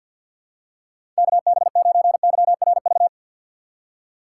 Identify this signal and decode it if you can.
Morse code, heard as a keyed tone, about 51 wpm, 700 Hz. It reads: KB9YRV
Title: USA Call Signs - 6 Characters - 4x Character Spacing - 50wpm